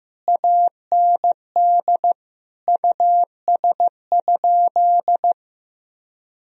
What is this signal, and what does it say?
Morse code, heard as a keyed tone, about 15 wpm, 700 Hz. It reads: AND US?
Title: Sentences from Top 1000 Words - Review Part 3 - 15wpm